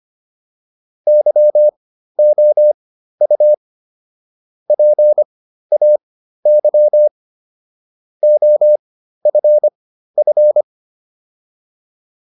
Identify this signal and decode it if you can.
Morse code, heard as a keyed tone, about 25 wpm, spaced as for 13 wpm, 600 Hz. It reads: YOU PAY OFF